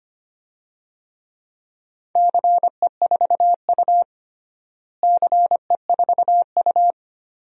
Morse code, heard as a keyed tone, about 25 wpm, 700 Hz. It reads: CE4U CE4U
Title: DXCC-100 Call Signs - 4 Characters - Repeated 2x - 25wpm